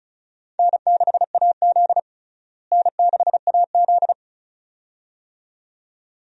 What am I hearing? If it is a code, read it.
Morse code, heard as a keyed tone, about 36 wpm, 700 Hz. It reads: N6AZ N6AZ
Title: USA Call Signs - 4 Characters - Repeated 2x - 35wpm